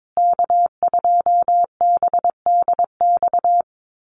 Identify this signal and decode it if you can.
Morse code, heard as a keyed tone, about 22 wpm, 700 Hz. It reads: K2BDX
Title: USA Call Signs Repeated 2x 22wpm